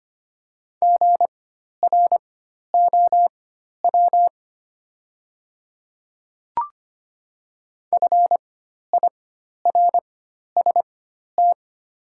Morse code, heard as a keyed tone, about 25 wpm, spaced as for 12 wpm, 700 Hz. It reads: GROW FIRST
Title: 4x Character Spacing - Top 300 Words - 25wpm